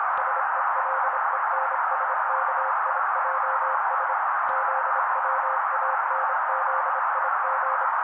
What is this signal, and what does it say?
Morse code, heard as a keyed tone, about 25 wpm, 530 Hz. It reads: SELENSKIJS ZWANZIG
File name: daily / 25wpm noise